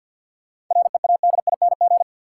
Morse code, heard as a keyed tone, about 52 wpm, 700 Hz. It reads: READING
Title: Sentences from Top 600 Words - New Word Drill - 50wpm